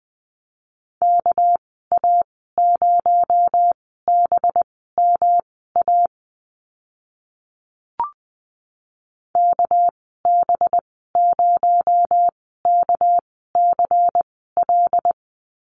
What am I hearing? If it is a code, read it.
Morse code, heard as a keyed tone, about 20 wpm, spaced as for 14 wpm, 700 Hz. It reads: KA0BMA KB0KCL
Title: 219 Lesson 35g - Number 0 - Call Signs 20wpm